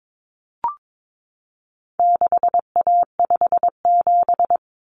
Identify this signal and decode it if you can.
Morse code, heard as a keyed tone, about 22 wpm, 700 Hz. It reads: 6A57
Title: Common Cut Numbers - 4-Digit 22wpm